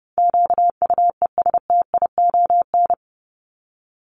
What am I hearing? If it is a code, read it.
Morse code, heard as a keyed tone, about 30 wpm, 700 Hz. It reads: QUESTION
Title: ES:S Suffix - Top 100 Words with Base Words 30wpm